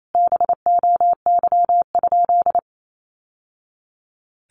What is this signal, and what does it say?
Morse code, heard as a keyed tone, about 28 wpm, 700 Hz. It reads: BOY?
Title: Sentences from Top 1100 Words 28wpm